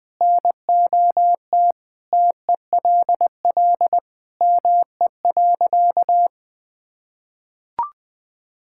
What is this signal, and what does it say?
Morse code, heard as a keyed tone, about 20 wpm, 700 Hz. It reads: NOT TELL ME.
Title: Sentences with 3-Word Phrases - 20wpm